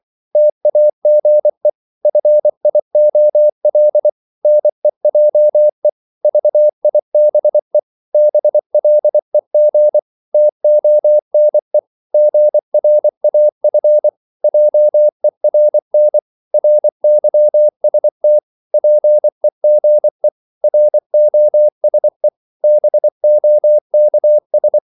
Morse code, heard as a keyed tone, about 24 wpm, 600 Hz. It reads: TAGE FIOL NEJE VIBE BLEG TONE GRAF JERN RYST PEGE ROSE BOKS
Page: Korte danske ord 24wpm